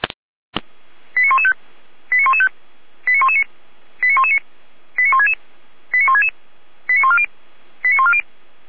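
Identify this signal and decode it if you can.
Five-tone selective call (ZVEI), heard as quick runs of five tones, tones of about 70 ms. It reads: 89115 89115 89119 89119 89166 89166 89140 89140
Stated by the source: ZVEI five-tone selective call received by SDR.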